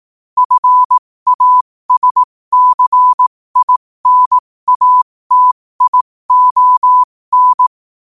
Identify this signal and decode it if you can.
Morse code, heard as a keyed tone, about 18 wpm, spaced as for 15 wpm, 1000 Hz. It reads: FASCINATION
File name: qotdfiles / 15wpm